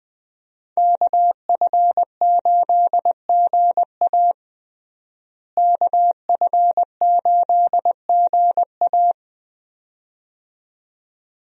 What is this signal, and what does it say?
Morse code, heard as a keyed tone, about 20 wpm, 700 Hz. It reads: KF8GA KF8GA
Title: USA Call Signs - 5 Characters - Repeated 2x - 20wpm